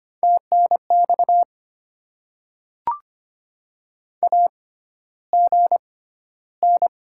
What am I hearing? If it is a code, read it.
Morse code, heard as a keyed tone, about 25 wpm, spaced as for 13 wpm, 700 Hz. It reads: TNX AGN
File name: Top 20 QSO Elements - 6x Character Spacing - 25wpm